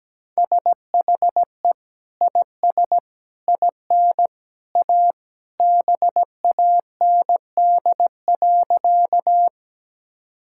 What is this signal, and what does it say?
Morse code, heard as a keyed tone, about 17 wpm, 700 Hz. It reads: SHE IS IN A BAND.
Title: Sentences from Top 1600 Words 17wpm